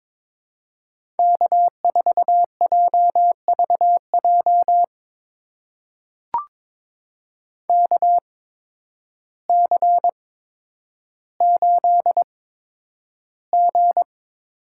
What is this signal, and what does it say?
Morse code, heard as a keyed tone, about 22 wpm, spaced as for 10 wpm, 700 Hz. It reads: K4JVJ KC8G
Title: USA Call Signs - 8x Character Spacing - 22wpm